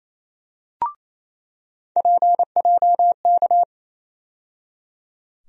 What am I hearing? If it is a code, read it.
Morse code, heard as a keyed tone, about 28 wpm, 700 Hz. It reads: PJK
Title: USA Call Sign Postfixes - 3 Letters 28wpm